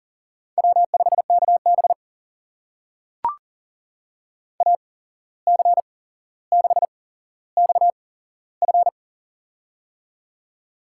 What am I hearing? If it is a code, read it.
Morse code, heard as a keyed tone, about 40 wpm, spaced as for 16 wpm, 700 Hz. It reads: W5KB AC6XF
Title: USA Call Signs - 8x Character Spacing - 40wpm